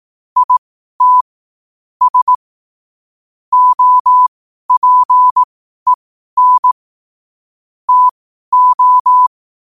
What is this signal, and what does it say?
Morse code, heard as a keyed tone, about 18 wpm, spaced as for 12 wpm, 1000 Hz. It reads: IT S OPEN TO
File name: qotdfiles / 12wpm